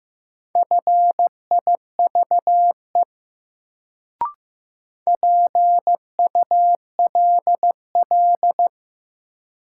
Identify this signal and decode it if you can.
Morse code, heard as a keyed tone, about 15 wpm, 700 Hz. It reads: FIVE PULL
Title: Sentences from Top 400 Words - New Word Drill - 15wpm